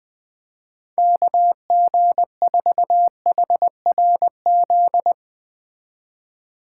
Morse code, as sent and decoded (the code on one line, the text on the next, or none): -.- --. ....- .... .-. --..
KG4HRZ